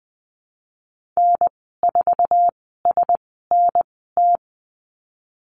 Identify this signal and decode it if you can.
Morse code, heard as a keyed tone, about 20 wpm, spaced as for 15 wpm, 700 Hz. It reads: N4SNT
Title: 055 Lesson 10g - Letter r - Call Signs 20wpm